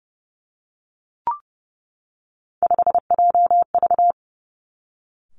Morse code, heard as a keyed tone, about 30 wpm, 700 Hz. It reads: 5JV
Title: Three Challenging Letter-Number Combinations 30wpm